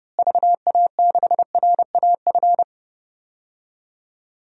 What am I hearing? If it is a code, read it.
Morse code, heard as a keyed tone, about 30 wpm, 700 Hz. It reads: VA6RAF
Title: DXCC-100 Call Signs - 6 Characters - Repeated 2x - 30wpm